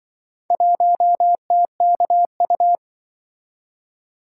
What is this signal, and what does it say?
Morse code, heard as a keyed tone, about 24 wpm, 700 Hz. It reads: 1TKU
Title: Speed-Racing - Four Letter-Number Combinations - 20wpm